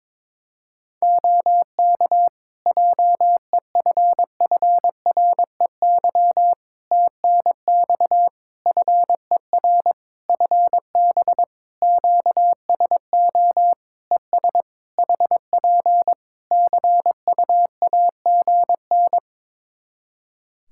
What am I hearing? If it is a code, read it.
Morse code, heard as a keyed tone, about 22 wpm, 700 Hz. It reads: OK JEFFREY TNX FER FB QSO ES HP CUAGN
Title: Practice QSO - Exchange 4 - Ending - 22wpm